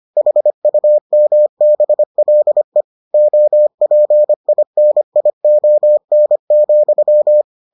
Morse code, heard as a keyed tone, about 25 wpm, 600 Hz. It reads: HUMBLE OPINION,